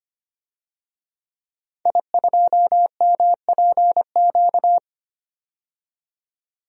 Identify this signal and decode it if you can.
Morse code, heard as a keyed tone, about 25 wpm, 700 Hz. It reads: I2MPQ
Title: DXCC-100 Call Signs - 5 Characters - 25wpm